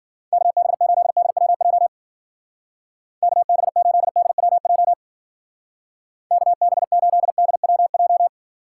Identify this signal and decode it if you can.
Morse code, heard as a keyed tone, about 49 wpm, 700 Hz. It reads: KB8DWJ KB8DWJ KB8DWJ
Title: Speed Racing - USA 2x3 Call Signs 40wpm